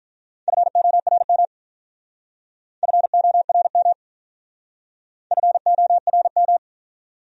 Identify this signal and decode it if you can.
Morse code, heard as a keyed tone, about 44 wpm, 700 Hz. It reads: FORM FORM FORM